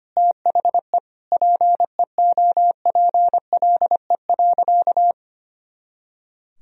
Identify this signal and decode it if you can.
Morse code, heard as a keyed tone, about 25 wpm, 700 Hz. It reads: THE PEOPLE.
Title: Sentences from Top 400 Words and Abbreviations 25wpm